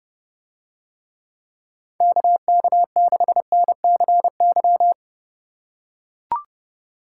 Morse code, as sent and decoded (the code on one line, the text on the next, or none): -.- -.- -.... -. -.-. -.--
KK6NCY